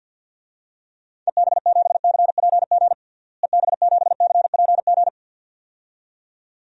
Morse code, heard as a keyed tone, about 52 wpm, 700 Hz. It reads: EB7CPG EB7CPG
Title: DXCC-100 Call Signs - 6 Characters - Repeated 2x - 50wpm